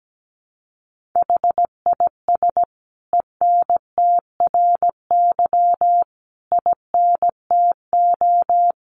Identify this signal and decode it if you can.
Morse code, heard as a keyed tone, about 17 wpm, 700 Hz. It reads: HIS ENTRY INTO
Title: Sentences from Top 1500 Words 17wpm